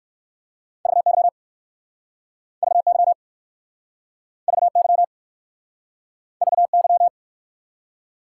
Morse code, heard as a keyed tone, about 59 wpm, 700 Hz. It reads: VY VY VY VY